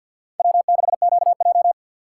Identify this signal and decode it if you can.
Morse code, heard as a keyed tone, about 52 wpm, 700 Hz. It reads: W6QJ
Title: USA Call Signs - 4x Character Spacing - 50wpm